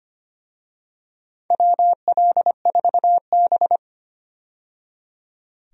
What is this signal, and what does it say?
Morse code, heard as a keyed tone, about 25 wpm, 700 Hz. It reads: WL4B
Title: USA Call Signs - 4 Characters - 25wpm